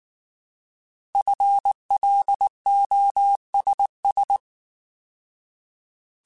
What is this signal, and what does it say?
Morse code, heard as a keyed tone, about 19 wpm, 800 Hz. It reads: FLOSS
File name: Alphabetical / Alphabetical mp3s / Fs @18